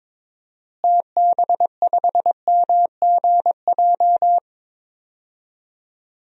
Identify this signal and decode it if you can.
Morse code, heard as a keyed tone, about 22 wpm, 700 Hz. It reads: TB5MGJ
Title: DXCC-100 Call Signs - 6 Characters - Repeated 3x - 22wpm